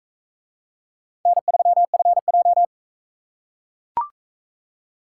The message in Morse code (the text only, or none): N3FJ